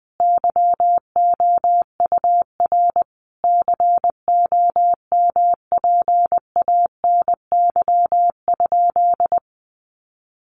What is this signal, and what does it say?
Morse code, heard as a keyed tone, about 19 wpm, 700 Hz. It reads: YOUR COMPANY?
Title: Sentences from Top 600 Words 20wpm